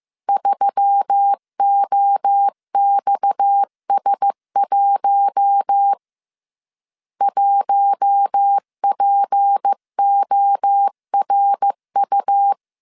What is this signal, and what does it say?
Morse code, heard as a keyed tone, about 15 wpm, 800 Hz. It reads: 3OXS1 1PORU
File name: CW / DET / 1 6 1500C S C 14wpm 2951